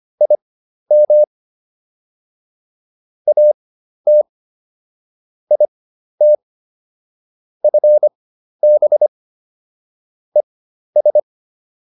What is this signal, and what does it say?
Morse code, heard as a keyed tone, about 25 wpm, spaced as for 12 wpm, 600 Hz. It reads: IM AT IT FB ES